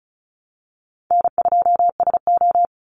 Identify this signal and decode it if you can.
Morse code, heard as a keyed tone, about 34 wpm, 700 Hz. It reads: N2SO
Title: Call Signs - 35wpm